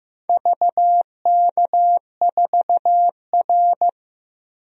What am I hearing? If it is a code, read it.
Morse code, heard as a keyed tone, about 15 wpm, 700 Hz. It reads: VK4R
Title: DXCC-100 Call Signs - 4 Characters - 15wpm